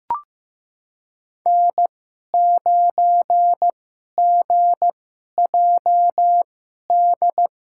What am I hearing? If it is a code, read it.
Morse code, heard as a keyed tone, about 15 wpm, spaced as for 11 wpm, 700 Hz. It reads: N9GJD
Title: USA Call Signs - 5 Characters - 2x Character Spacing - 15wpm